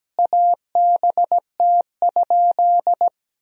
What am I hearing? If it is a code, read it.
Morse code, heard as a keyed tone, about 17 wpm, 700 Hz. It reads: ABT?
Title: Sentences from Top 100 Words and Abbreviations 17wpm